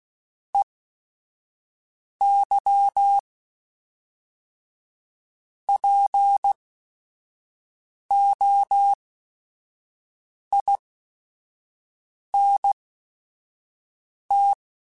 Morse code, heard as a keyed tone, about 16 wpm, spaced as for 5 wpm, 800 Hz. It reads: EY POINT
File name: Technical Subjects / Technical Subjects-mp3s / Reactance @5